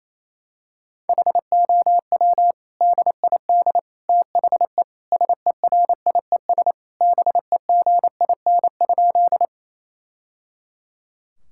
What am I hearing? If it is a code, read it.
Morse code, heard as a keyed tone, about 28 wpm, 700 Hz. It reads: HOW DID THE SERIES BEGIN?
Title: Sentences from Top 1000 Words - Review Part 5 - 28wpm